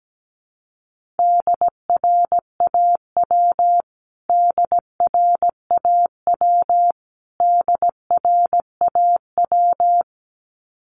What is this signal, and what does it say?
Morse code, heard as a keyed tone, about 17 wpm, 700 Hz. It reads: DRAW DRAW DRAW
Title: Top 500 Words Repeated 17wpm